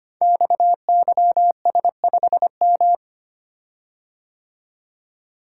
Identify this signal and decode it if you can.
Morse code, heard as a keyed tone, about 25 wpm, 700 Hz. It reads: XYS5M
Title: Code Groups - 5 Letter:Numbers - 25wpm